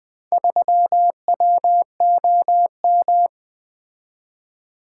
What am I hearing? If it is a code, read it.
Morse code, heard as a keyed tone, about 21 wpm, 700 Hz. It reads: 3WOM
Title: Four Letter-Number Combinations - 20wpm